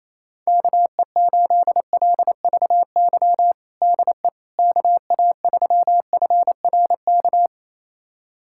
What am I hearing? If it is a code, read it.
Morse code, heard as a keyed tone, about 28 wpm, 700 Hz. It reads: KE8LVY DE KA3FRK
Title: USA Callsign DE Callsign - 6 Characters - 28wpm